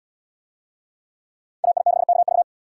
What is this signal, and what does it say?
Morse code, heard as a keyed tone, about 77 wpm, 700 Hz. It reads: AI6KV